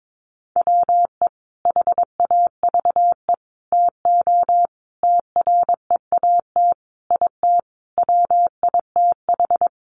Morse code, heard as a keyed tone, about 22 wpm, 700 Hz. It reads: WE HAVE TO TREAT IT WITH
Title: Sentences from Top 1300 Words 22wpm